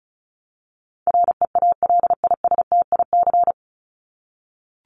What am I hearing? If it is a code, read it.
Morse code, heard as a keyed tone, about 34 wpm, 700 Hz. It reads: REALISTIC
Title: Top 1751-2000 Words 35wpm